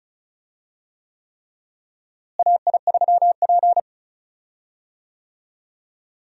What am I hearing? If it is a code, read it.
Morse code, heard as a keyed tone, about 34 wpm, 700 Hz. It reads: AI3P
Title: USA Call Signs - 4 Characters - Repeated 3x - 35wpm